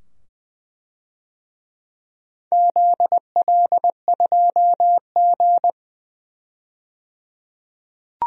Morse code, heard as a keyed tone, about 20 wpm, 700 Hz. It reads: ZL2G
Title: DXCC-100 Call Signs - 4 Characters - 20wpm